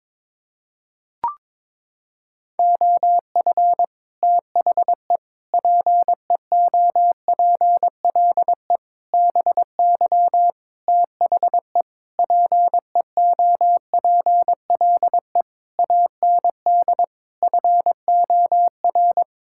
Morse code, heard as a keyed tone, about 22 wpm, 700 Hz. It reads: OF THE PEOPLE BY THE PEOPLE AND FOR
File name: practice / 22wpm / Sets of 8 Words - 22wpm